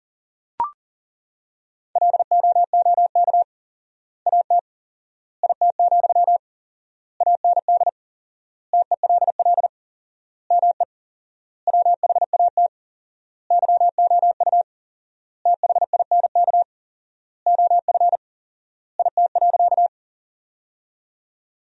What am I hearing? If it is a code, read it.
Morse code, heard as a keyed tone, about 37 wpm, 700 Hz. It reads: LOOK AT IT, AND TELL ME WHAT YOU THINK OF IT.
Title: 4x Word-Spacing - Sentences from Top 200 Words - 40wpm